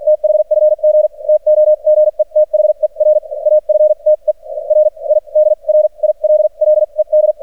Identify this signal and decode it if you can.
Morse code, heard as a keyed tone, about 44 wpm, spaced as for 35 wpm, 610 Hz. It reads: THFG TOGETHER AFTER I DRILLED
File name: archived / CRSnet